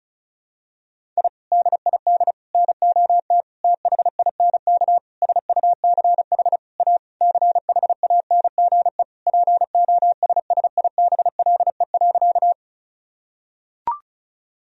Morse code, heard as a keyed tone, about 36 wpm, 700 Hz. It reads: I DID NOT THINK SUCH A CHANGE POSSIBLE.
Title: Sentences from Top 1000 Words - Review Part 1 - 35wpm